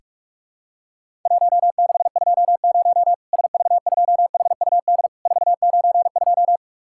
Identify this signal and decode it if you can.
Morse code, heard as a keyed tone, about 46 wpm, 700 Hz. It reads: 1620 SV2HUD 492
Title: Worked All Europe DX Contest - QTC Exchange - 45wpm